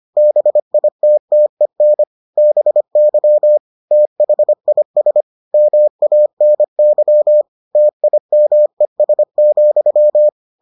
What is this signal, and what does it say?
Morse code, heard as a keyed tone, about 25 wpm, 600 Hz. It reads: BITTEN BY THIS MANY TIMES,